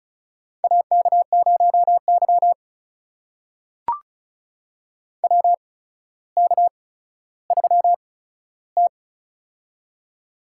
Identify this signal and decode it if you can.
Morse code, heard as a keyed tone, about 36 wpm, spaced as for 15 wpm, 700 Hz. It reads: AK0Y WK3T
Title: USA Call Signs - 4 Characters - 8x Character Spacing - 35wpm